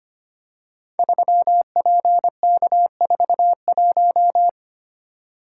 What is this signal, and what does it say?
Morse code, heard as a keyed tone, about 25 wpm, 700 Hz. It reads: 3PK41
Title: Code Groups - 5 Letter:Numbers - 25wpm